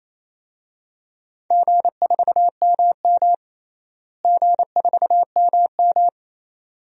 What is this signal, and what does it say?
Morse code, heard as a keyed tone, about 27 wpm, 700 Hz. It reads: G4MM G4MM
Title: DXCC-100 Call Signs - 4 Characters - Repeated 2x - 28wpm